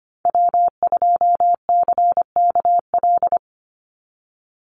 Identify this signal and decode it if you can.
Morse code, heard as a keyed tone, about 25 wpm, 700 Hz. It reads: W2CKL
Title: NA SOTA Activator Callsigns 25wpm